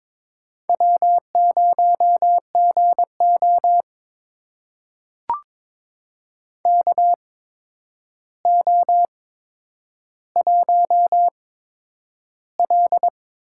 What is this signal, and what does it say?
Morse code, heard as a keyed tone, about 22 wpm, spaced as for 9 wpm, 700 Hz. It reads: W0GO KO1L
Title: USA Call Signs - 4 Characters - 8x Character Spacing - 22wpm